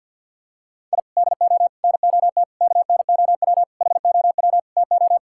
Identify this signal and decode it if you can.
Morse code, heard as a keyed tone, about 50 wpm, 700 Hz. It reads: I DO NOT KNOW HOW TO
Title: Sentences from Top 1000 Words - Review Part 6 - 50wpm